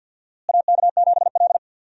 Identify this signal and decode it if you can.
Morse code, heard as a keyed tone, about 52 wpm, 700 Hz. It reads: AK7L
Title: USA Call Signs - 4 Characters - Repeated 3x - 50wpm